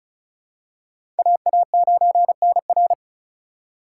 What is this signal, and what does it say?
Morse code, heard as a keyed tone, about 34 wpm, 690 Hz. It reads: AA9NR
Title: USA Call Signs - 5 Characters - 2x Character Spacing - 35wpm